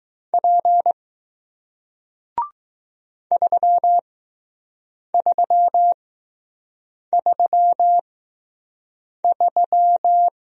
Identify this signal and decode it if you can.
Morse code, heard as a keyed tone, about 20 wpm, spaced as for 6 wpm, 700 Hz. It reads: P 3333